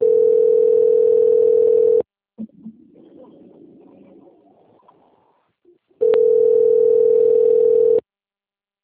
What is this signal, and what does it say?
Ringback tone (North America), 440 + 480 Hz.